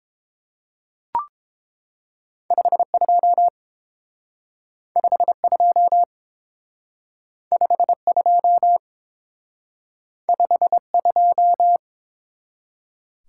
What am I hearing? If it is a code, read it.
Morse code, heard as a keyed tone, about 29 wpm, 700 Hz. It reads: 52 52 52 52